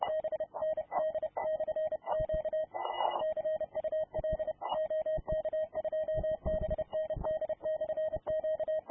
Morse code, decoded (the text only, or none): BND/Q CUL OK2BND/Q